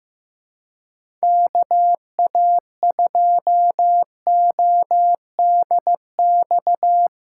Morse code, heard as a keyed tone, about 15 wpm, 700 Hz. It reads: KA2ODX
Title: USA Callsign DE Callsign - 6 Characters - 15wpm